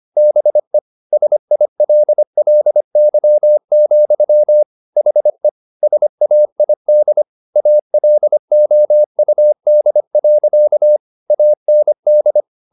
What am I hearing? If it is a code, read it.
Morse code, heard as a keyed tone, about 25 wpm, 600 Hz. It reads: BE SILLY, HE SAID ALOUD. AND